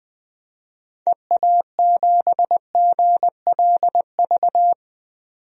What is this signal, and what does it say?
Morse code, heard as a keyed tone, about 20 wpm, 700 Hz. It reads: EA7GLV
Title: DXCC-100 Call Signs - 6 Characters - 20wpm